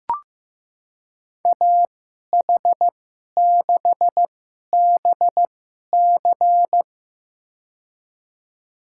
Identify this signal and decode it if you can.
Morse code, heard as a keyed tone, about 15 wpm, spaced as for 11 wpm, 700 Hz. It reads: AH6BC
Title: USA Call Signs - 2x Character Spacing - 15wpm